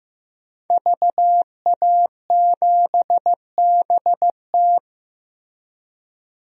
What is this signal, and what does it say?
Morse code, heard as a keyed tone, about 16 wpm, 700 Hz. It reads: VA7BT
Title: DXCC-100 Call Signs - 5 Characters - Repeated 2x - 15wpm